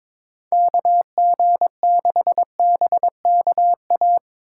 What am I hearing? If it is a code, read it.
Morse code, heard as a keyed tone, about 25 wpm, 700 Hz. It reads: KG6BKA
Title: Speed Racing - USA 2x3 Call Signs 22wpm